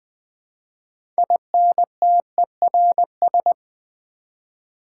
Morse code, heard as a keyed tone, about 20 wpm, 700 Hz. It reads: INTERS
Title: Build A Word - 6 Letters At A Time - 12-Letter Words - 20wpm